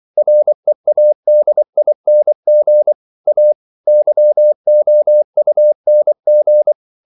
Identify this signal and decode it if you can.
Morse code, heard as a keyed tone, about 24 wpm, 600 Hz. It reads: READING A YOUNG
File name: CW / Practice / EnglishQuotations0424